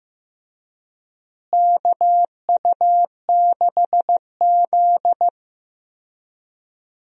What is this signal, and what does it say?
Morse code, heard as a keyed tone, about 15 wpm, 700 Hz. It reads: KU6Z
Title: USA Call Signs - 4 Characters - Repeated 3x - 15wpm